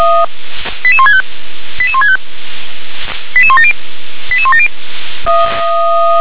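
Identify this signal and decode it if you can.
Five-tone selective call (ZVEI), heard as quick runs of five tones, tones of about 70 ms. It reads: 88165 88165 88170 88170
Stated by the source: ZVEI five-tone selective call 88051 received by SDR.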